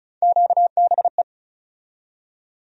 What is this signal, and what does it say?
Morse code, heard as a keyed tone, about 36 wpm, 700 Hz. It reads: QBE
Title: USA Call Sign Postfixes - 3 Letters 35wpm